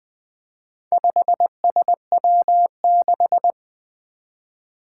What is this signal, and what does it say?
Morse code, heard as a keyed tone, about 20 wpm, 700 Hz. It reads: 5SW6
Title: Four Challenging Letter-Number Combinations 20wpm